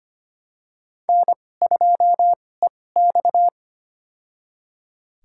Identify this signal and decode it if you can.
Morse code, heard as a keyed tone, about 25 wpm, spaced as for 18 wpm, 700 Hz. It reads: N2EX
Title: USA Call Signs - 4 Characters - 2x Character Spacing - 25wpm